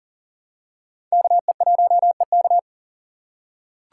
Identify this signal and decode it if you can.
Morse code, heard as a keyed tone, about 40 wpm, 700 Hz. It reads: KE1EK
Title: USA Call Signs - 5 Characters - 4x Character Spacing - 40wpm